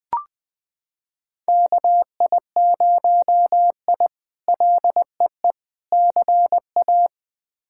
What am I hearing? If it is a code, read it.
Morse code, heard as a keyed tone, about 20 wpm, 700 Hz. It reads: KI0I LEE CA
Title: K1USN SST QSO Practice - 20wpm